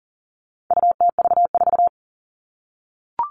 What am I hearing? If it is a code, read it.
Morse code, heard as a keyed tone, about 37 wpm, 700 Hz. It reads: UTV4
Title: All Cut Numbers - 4-Digit 40wpm